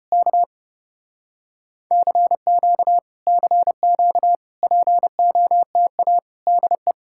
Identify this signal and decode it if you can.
Morse code, heard as a keyed tone, about 34 wpm, 700 Hz. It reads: K CQ CQ POTA DE